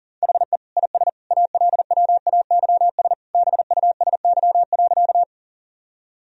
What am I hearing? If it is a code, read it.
Morse code, heard as a keyed tone, about 40 wpm, 700 Hz. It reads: HE IS ALWAYS BUSY.
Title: Sentences from Top 1000 Words - Review Part 3 - 40wpm